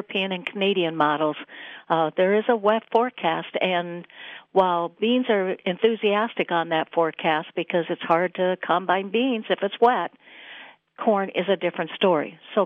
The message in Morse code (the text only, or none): none